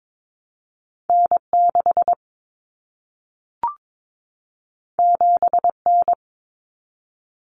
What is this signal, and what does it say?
Morse code, heard as a keyed tone, about 22 wpm, 700 Hz. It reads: N6 7N
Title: Common Cut Numbers - 2-Digit 22wpm